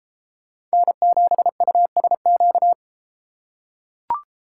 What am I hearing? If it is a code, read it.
Morse code, heard as a keyed tone, about 31 wpm, 700 Hz. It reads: N7USQ